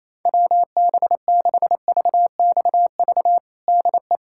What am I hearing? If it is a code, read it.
Morse code, heard as a keyed tone, about 28 wpm, 700 Hz. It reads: WB6VXV DE
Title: USA Callsign DE Callsign 28wpm